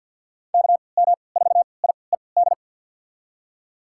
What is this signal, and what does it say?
Morse code, heard as a keyed tone, about 52 wpm, spaced as for 29 wpm, 700 Hz. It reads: KM4IED
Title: USA Call Signs - 6 Characters - 3x Character Spacing - 50wpm